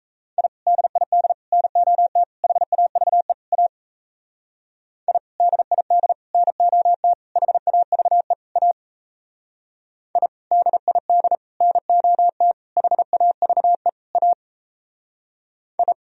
Morse code, heard as a keyed tone, about 38 wpm, 700 Hz. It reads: I DID NOT HAVE A I DID NOT HAVE A I DID NOT HAVE A I